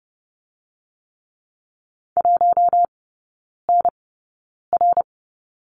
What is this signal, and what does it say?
Morse code, heard as a keyed tone, about 30 wpm, spaced as for 9 wpm, 700 Hz. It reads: 1NR
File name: course / 30:10wpm / 087 Lesson 15e - Number 5 - Three Characters 30wpm